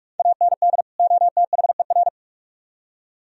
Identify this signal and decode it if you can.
Morse code, heard as a keyed tone, about 46 wpm, 700 Hz. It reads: AND OTHER